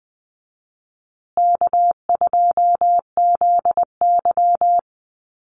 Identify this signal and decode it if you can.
Morse code, heard as a keyed tone, about 20 wpm, 700 Hz. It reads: K2ZY
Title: USA Call Signs Repeated 2x 20wpm